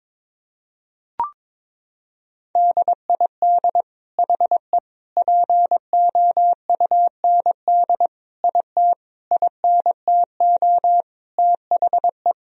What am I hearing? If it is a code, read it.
Morse code, heard as a keyed tone, about 22 wpm, 700 Hz. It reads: DID HE POUND IT INTO THE
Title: Sentences from Top 1000 Words - Review Part 2 - 22wpm